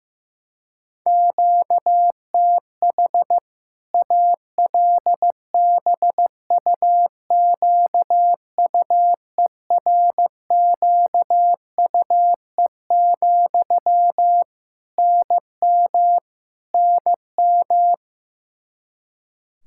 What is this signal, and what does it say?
Morse code, heard as a keyed tone, about 15 wpm, 700 Hz. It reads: QTH ALBUQUERQUE, NM NM
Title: Practice QSO - Exchange 1 - Beginning - 15wpm